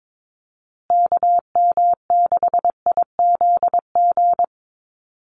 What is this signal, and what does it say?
Morse code, heard as a keyed tone, about 22 wpm, 700 Hz. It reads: KM6IZG
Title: USA Call Signs Repeated 3x 22wpm